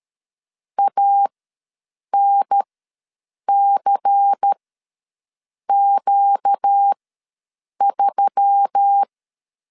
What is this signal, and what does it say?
Morse code, heard as a keyed tone, about 13 wpm, spaced as for 7 wpm, 800 Hz. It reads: ANCQ3
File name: LEZIONI / 5 LBVC / 5 LBVC ws 12wpm 3159